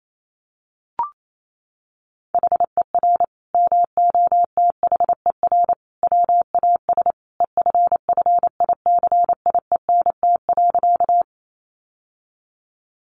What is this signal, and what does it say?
Morse code, heard as a keyed tone, about 28 wpm, 700 Hz. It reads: HER MOTHER WAS EFFICIENT.
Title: Sentences from Top 1300 Words 28wpm